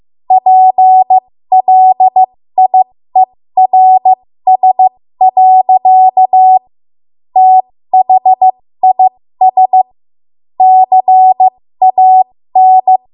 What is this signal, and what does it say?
Morse code, heard as a keyed tone, about 15 wpm, spaced as for 13 wpm, 750 Hz. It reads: PLIERS. THIS CAN
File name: Morse / 13WPM